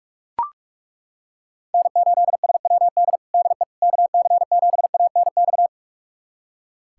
Morse code, heard as a keyed tone, about 41 wpm, 690 Hz. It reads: N8SWD DE KC7ANX
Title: USA Callsign DE Callsign 45wpm